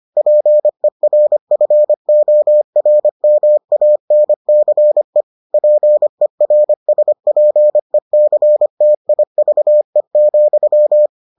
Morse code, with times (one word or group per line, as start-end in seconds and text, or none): 0.2-5.2 PERFORMANCE
5.5-11.1 PERSPECTIVE,